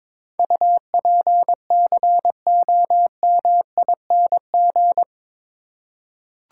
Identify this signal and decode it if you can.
Morse code, heard as a keyed tone, about 22 wpm, 700 Hz. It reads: UPCOMING
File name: practice / 22wpm / Eight Letter Words - 22wpm